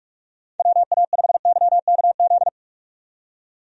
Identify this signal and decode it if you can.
Morse code, heard as a keyed tone, about 46 wpm, 700 Hz. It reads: WA5YKZ
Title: USA Call Signs - 6 Characters - 45wpm